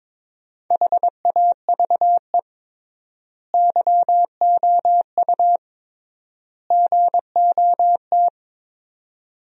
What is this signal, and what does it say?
Morse code, heard as a keyed tone, about 22 wpm, 700 Hz. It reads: HAVE YOU GOT
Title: 3x Word-Spacing - Sentences from Top 300 Words - 22wpm